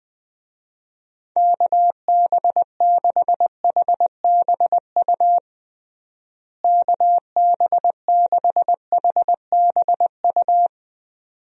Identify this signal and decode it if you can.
Morse code, heard as a keyed tone, about 20 wpm, 700 Hz. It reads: KB6HBU KB6HBU
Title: USA Call Signs - 6 Characters - Repeated 2x - 20wpm